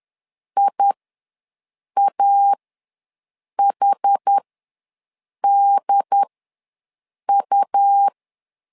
Morse code, heard as a keyed tone, about 10 wpm, spaced as for 6 wpm, 800 Hz. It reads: IAHDU
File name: LEZIONI / 2 UHDN / 2 UHDN ws 10wpm 2044